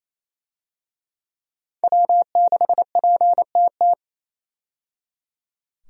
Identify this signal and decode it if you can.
Morse code, heard as a keyed tone, about 28 wpm, 700 Hz. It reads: W6PTT
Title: USA Call Signs - 5 Characters - 28wpm